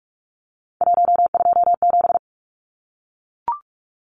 Morse code, heard as a keyed tone, about 44 wpm, 700 Hz. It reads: 127
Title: Numbers - 3-Digit 45wpm